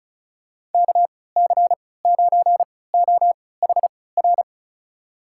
Morse code, heard as a keyed tone, about 34 wpm, spaced as for 20 wpm, 700 Hz. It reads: KC9OHR
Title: USA Call Signs - 6 Characters - 3x Character Spacing - 35wpm